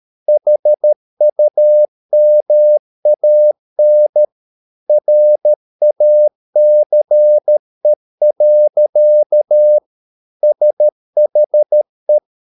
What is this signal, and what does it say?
Morse code, heard as a keyed tone, about 13 wpm, 600 Hz. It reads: HUMAN RACE. SHE